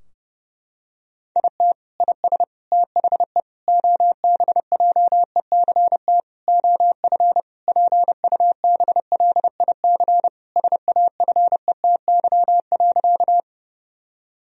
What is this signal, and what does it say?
Morse code, heard as a keyed tone, about 32 wpm, 700 Hz. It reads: IT IS THE OBJECT OF PUBLIC SAFETY.
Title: Sentences from Top 1000 Words - Review Part 4 - 30wpm